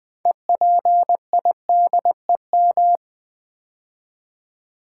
Morse code, heard as a keyed tone, about 20 wpm, 700 Hz. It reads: EPIDEM
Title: Build A Word - 2 Letters At A Time - 8-Letter Words - 20wpm